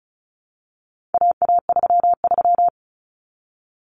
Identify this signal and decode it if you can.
Morse code, heard as a keyed tone, about 34 wpm, 700 Hz. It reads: AA33
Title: Common Cut Numbers - 4-Digit 35wpm